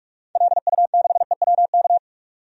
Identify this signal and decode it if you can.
Morse code, heard as a keyed tone, about 46 wpm, 700 Hz. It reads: LU6EWK
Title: DXCC-100 Call Signs - 6 Characters - 45wpm